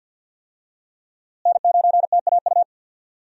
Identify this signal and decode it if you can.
Morse code, heard as a keyed tone, about 49 wpm, 700 Hz. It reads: N9TAU
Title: USA Call Signs - 5 Characters - Repeated 2x - 50wpm